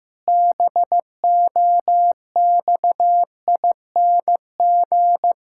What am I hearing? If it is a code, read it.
Morse code, heard as a keyed tone, about 15 wpm, 700 Hz. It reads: BOXING
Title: Pangram Challenge v2 - 15wpm